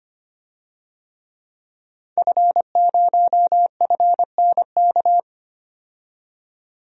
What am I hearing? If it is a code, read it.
Morse code, heard as a keyed tone, about 25 wpm, 690 Hz. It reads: F0FNK
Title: DXCC-100 Call Signs - 5 Characters - 25wpm